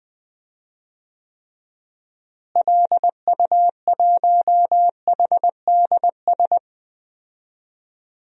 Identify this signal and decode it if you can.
Morse code, heard as a keyed tone, about 20 wpm, 700 Hz. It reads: LU1HDS
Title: DXCC-100 Call Signs - 6 Characters - Repeated 3x - 20wpm